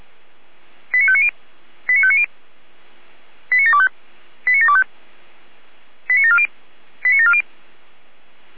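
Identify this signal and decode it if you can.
Five-tone selective call (ZVEI), heard as quick runs of five tones, tones of about 70 ms. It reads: 89590 89590 89725 89725 89740 89740
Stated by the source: Five-tone ZVEI selective call received on an SDR.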